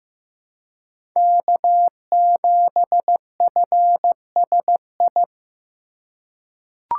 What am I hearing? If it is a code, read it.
Morse code, heard as a keyed tone, about 15 wpm, 700 Hz. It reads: K7FSI
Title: USA Call Signs - 5 Characters - 2x Character Spacing - 15wpm